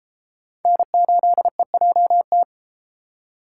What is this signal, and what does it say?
Morse code, heard as a keyed tone, about 33 wpm, 700 Hz. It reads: N8EJT